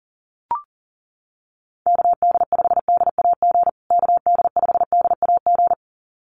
Morse code, heard as a keyed tone, about 40 wpm, 700 Hz. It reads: KD5DAG KD5DAG
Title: USA Call Signs Repeated 2x 40wpm